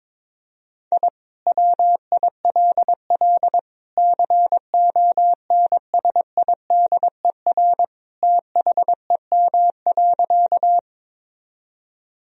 Morse code, heard as a keyed tone, about 22 wpm, 700 Hz. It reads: I WILL CONSIDER THEM.
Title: Sentences from Top 1000 Words - Review Part 4 - 22wpm